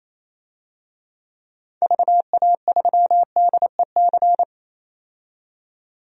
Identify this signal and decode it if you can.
Morse code, heard as a keyed tone, about 28 wpm, 700 Hz. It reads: VA3DEC
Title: DXCC-100 Call Signs - 6 Characters - 28wpm